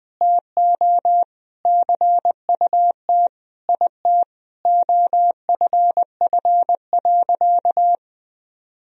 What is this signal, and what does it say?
Morse code, heard as a keyed tone, about 20 wpm, 700 Hz. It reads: TO CUT IT OFF.
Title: Sentences from Top 1000 Words - Review Part 1 - 20wpm